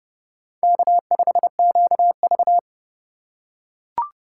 Morse code, heard as a keyed tone, about 30 wpm, 700 Hz. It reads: K5QV
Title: USA Call Signs - 4x Character Spacing - 30wpm